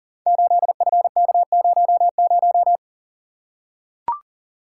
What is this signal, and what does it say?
Morse code, heard as a keyed tone, about 40 wpm, 690 Hz. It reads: 8FK00
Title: Code Groups - 5 Letter:Numbers - 40wpm